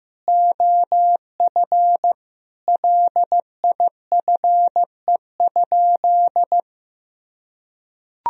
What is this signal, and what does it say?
Morse code, heard as a keyed tone, about 15 wpm, 700 Hz. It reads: OF LIFE?
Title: Sentences from Top 1000 Words - Review Part 3 - 15wpm